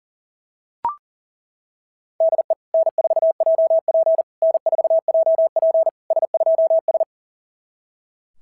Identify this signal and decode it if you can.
Morse code, heard as a keyed tone, about 40 wpm, 690 Hz. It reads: DE N4JP N4JP S2S
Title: SOTA QSO Practice 40wpm